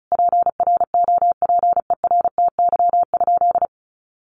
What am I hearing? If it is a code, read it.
Morse code, heard as a keyed tone, about 34 wpm, 700 Hz. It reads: PROPERTY?
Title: Sentences from Top 1500 Words 35wpm